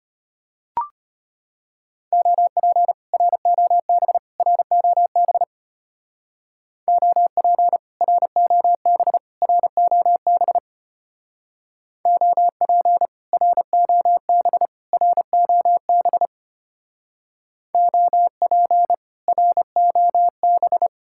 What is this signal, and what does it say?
Morse code, heard as a keyed tone, about 33 wpm, 700 Hz. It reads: OP ROB ROB OP ROB ROB OP ROB ROB OP ROB